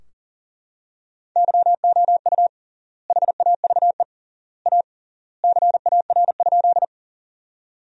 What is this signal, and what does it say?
Morse code, heard as a keyed tone, about 40 wpm, 700 Hz. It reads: YOU HAVE A CAR?
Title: 3x Word-Spacing - Sentences from Top 300 Words - 40wpm